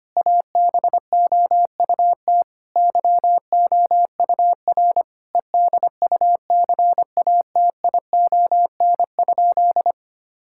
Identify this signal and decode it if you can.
Morse code, heard as a keyed tone, about 25 wpm, 700 Hz. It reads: ABOUT YOUR EDUCATION?
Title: Sentences from Top 1000 Words - Review Part 6 - 25wpm